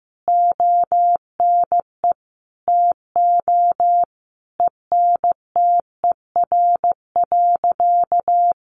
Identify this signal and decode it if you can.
Morse code, heard as a keyed tone, about 14 wpm, 700 Hz. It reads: ONE TO ENTER.
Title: Sentences from Top 1300 Words 15wpm